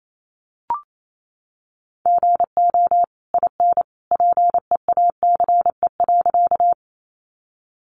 Morse code, heard as a keyed tone, about 28 wpm, 700 Hz. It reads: GO IN PEACE.
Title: Sentences from Top 1500 Words 28wpm